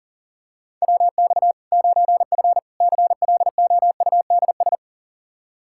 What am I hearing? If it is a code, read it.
Morse code, heard as a keyed tone, about 37 wpm, 700 Hz. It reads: WX 9F CLOUDS
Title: Practice QSO - Exchange 2 - Info - 40wpm